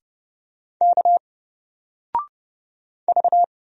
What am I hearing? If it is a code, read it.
Morse code, heard as a keyed tone, about 30 wpm, 700 Hz. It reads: K V
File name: Single-Letter-Number-30wpm